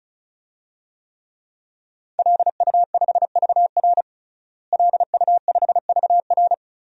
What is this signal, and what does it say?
Morse code, heard as a keyed tone, about 36 wpm, 700 Hz. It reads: LU5VR LU5VR
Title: DXCC-100 Call Signs - 5 Characters - Repeated 3x - 35wpm